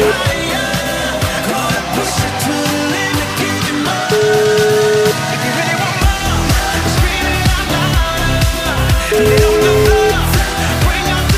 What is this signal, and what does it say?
Ringback tone (Europe, 425 Hz), with noise.